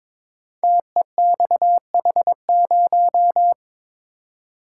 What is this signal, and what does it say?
Morse code, heard as a keyed tone, about 22 wpm, 700 Hz. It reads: TEXH0
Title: Code Groups - 5 Letter:Numbers - 22wpm